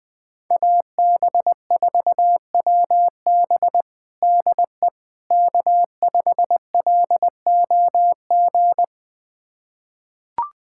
Morse code, heard as a keyed tone, about 20 wpm, 700 Hz. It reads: AB4WB DE K5LOG
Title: USA Callsign DE Callsign - 5 Characters - 20wpm